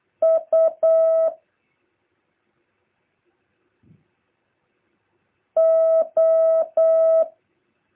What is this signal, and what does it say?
Morse code, heard as a keyed tone, about 8 wpm, 650 Hz. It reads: U O
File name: jh8cbh / 2023cw